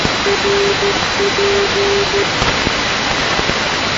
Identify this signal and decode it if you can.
Morse code, heard as a keyed tone, about 13 wpm, 400 Hz, with noise. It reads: RP
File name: asia / tkm / tkm-29-ndb dashoguz rp-322-2019-rb-uzb